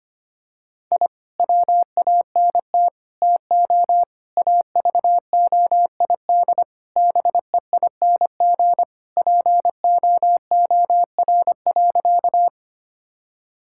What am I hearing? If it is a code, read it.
Morse code, heard as a keyed tone, about 25 wpm, 700 Hz. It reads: I WANT TO AVOID BEING POOR.
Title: Sentences from Top 1000 Words - Review Part 6 - 25wpm